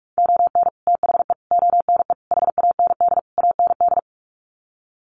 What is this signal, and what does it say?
Morse code, heard as a keyed tone, about 44 wpm, 700 Hz. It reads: ON THE ONE HAND AND
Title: Sets of 5 Words Encore2 45wpm